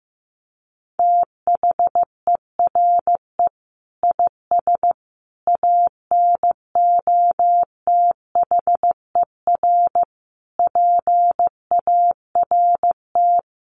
Morse code, heard as a keyed tone, about 15 wpm, 700 Hz. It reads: THERE IS ANOTHER PART
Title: Sentences from Top 1300 Words 15wpm